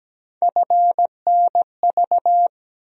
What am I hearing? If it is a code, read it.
Morse code, heard as a keyed tone, about 17 wpm, 700 Hz. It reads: FNV
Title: USA Call Sign Postfixes - 3 Letters 17wpm